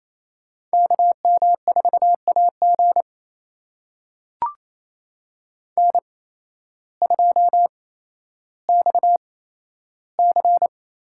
Morse code, heard as a keyed tone, about 28 wpm, spaced as for 13 wpm, 700 Hz. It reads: KM4AG N2XC
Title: USA Call Signs - 5 Characters - 8x Character Spacing - 28wpm